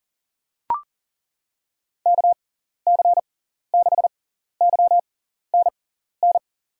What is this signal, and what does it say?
Morse code, heard as a keyed tone, about 40 wpm, spaced as for 14 wpm, 700 Hz. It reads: KC6YNN
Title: USA Call Signs - 6 Characters - 6x Character Spacing - 40wpm